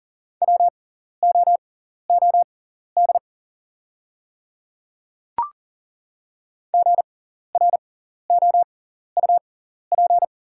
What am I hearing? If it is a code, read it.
Morse code, heard as a keyed tone, about 40 wpm, spaced as for 14 wpm, 700 Hz. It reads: WOOD GROUP
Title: 6x Character Spacing - Top 300 Words - 40wpm